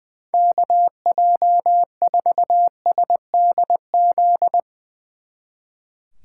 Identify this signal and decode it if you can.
Morse code, heard as a keyed tone, about 22 wpm, 700 Hz. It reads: KJ4SDZ
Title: Speed Racing - USA 2x3 Call Signs 20wpm